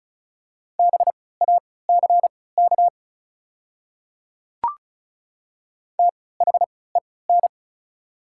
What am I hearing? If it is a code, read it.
Morse code, heard as a keyed tone, about 34 wpm, spaced as for 20 wpm, 700 Hz. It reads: BACK THEN
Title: 3x Character Spacing - Top 200 Words - 35wpm